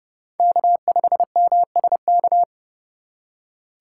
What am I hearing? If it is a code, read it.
Morse code, heard as a keyed tone, about 29 wpm, 700 Hz. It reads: K5MSK